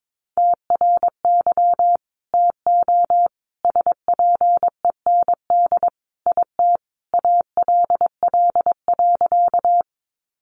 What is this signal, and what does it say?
Morse code, heard as a keyed tone, about 22 wpm, 700 Hz. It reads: TRY TO SPEND IT ALL.
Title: Sentences from Top 800 Words 22wpm